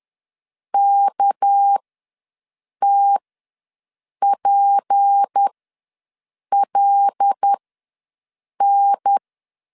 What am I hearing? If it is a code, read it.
Morse code, heard as a keyed tone, about 10 wpm, spaced as for 6 wpm, 800 Hz. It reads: KTPLN
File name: LEZIONI / 7 GF9Z / 7 G ws 10wpm 3843